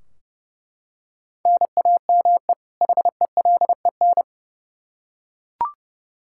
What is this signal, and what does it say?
Morse code, heard as a keyed tone, about 28 wpm, 700 Hz. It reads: NAME HELEN
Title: Practice QSO - Exchange 1 - Beginning - 30wpm